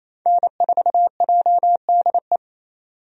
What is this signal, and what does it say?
Morse code, heard as a keyed tone, about 28 wpm, 700 Hz. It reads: N4JDE
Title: USA Call Signs - 5 Characters - 28wpm